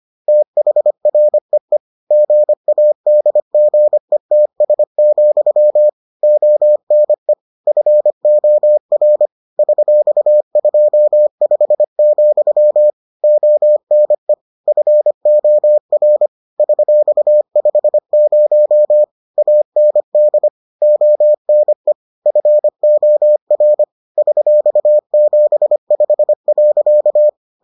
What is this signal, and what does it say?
Morse code, heard as a keyed tone, about 25 wpm, 600 Hz. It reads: THREE GADGETS, ONE FOR $25, ONE FOR $50 AND ONE FOR $75.